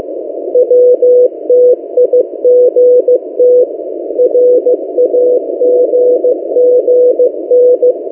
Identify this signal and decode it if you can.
Morse code, heard as a keyed tone, about 15 wpm, 500 Hz. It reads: WTIGT RAGGN
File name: M08a- -1355utc-8096khz